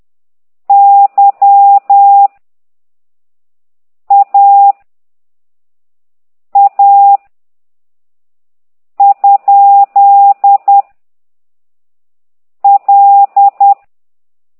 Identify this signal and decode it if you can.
Morse code, heard as a keyed tone, about 10 wpm, 800 Hz. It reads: Y A A ? L
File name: ct2idw / exerc9